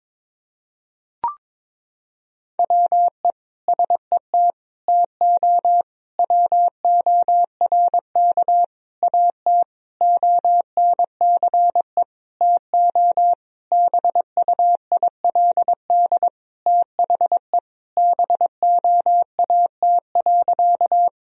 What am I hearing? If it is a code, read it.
Morse code, heard as a keyed tone, about 22 wpm, 700 Hz. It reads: WE SET TO WORK AT ONCE TO BUILD THE BOAT.
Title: Sentences from Top 1000 Words - Review Part 1 - 22wpm